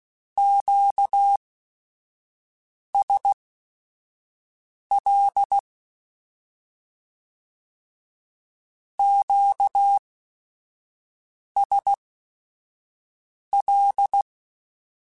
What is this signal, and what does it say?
Morse code, heard as a keyed tone, about 16 wpm, spaced as for 5 wpm, 800 Hz. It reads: QSL QSL
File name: Q Calls / Q Calls-mp3s / Q Calls Twice @5